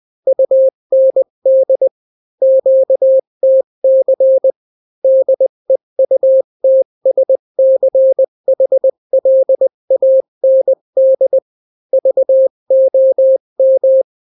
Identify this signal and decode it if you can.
Morse code, heard as a keyed tone, about 20 wpm, 530 Hz. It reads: UND QTC DEUTSCHLAND VOM
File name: daily / 18wpm